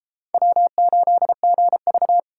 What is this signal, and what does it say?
Morse code, heard as a keyed tone, about 28 wpm, 700 Hz. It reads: W8GV